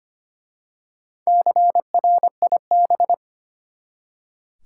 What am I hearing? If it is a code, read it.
Morse code, heard as a keyed tone, about 25 wpm, 700 Hz. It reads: CRIB
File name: practice / 25wpm / Easy Words with Hints - 25wpm